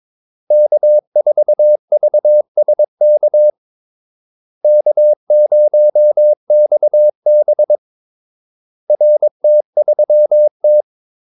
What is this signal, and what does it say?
Morse code, heard as a keyed tone, about 22 wpm, 600 Hz. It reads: K4VSK K0XB RT3T
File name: Callsigns-22wpm